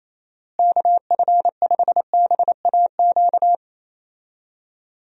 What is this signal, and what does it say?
Morse code, heard as a keyed tone, about 27 wpm, 700 Hz. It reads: KF5BAQ
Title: USA Call Signs - 6 Characters - Repeated 2x - 28wpm